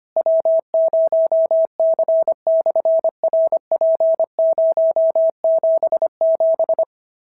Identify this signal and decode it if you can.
Morse code, heard as a keyed tone, about 25 wpm, 650 Hz. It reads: W0C/RP077
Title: SOTA QSO Practice 25wpm